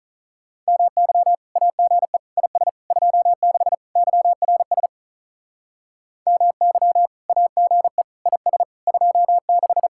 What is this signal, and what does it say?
Morse code, heard as a keyed tone, about 44 wpm, 700 Hz. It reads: MY AGE IS 26 YRS MY AGE IS 26
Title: Speed-Racing Common QSO Phrases - 35wpm